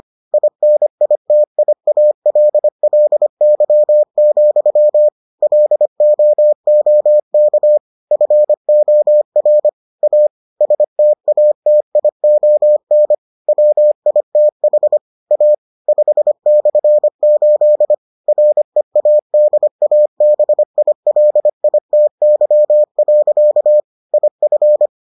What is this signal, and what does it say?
Morse code, heard as a keyed tone, about 25 wpm, 600 Hz. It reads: INITIALLY, LOOK FOR A STATION WITH A 5/8 READABILITY. IF